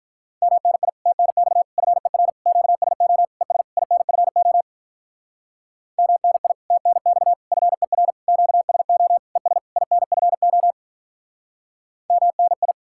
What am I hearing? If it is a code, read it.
Morse code, heard as a keyed tone, about 44 wpm, 700 Hz. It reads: MNI TNX FER QSO ES INFO MNI TNX FER QSO ES INFO MNI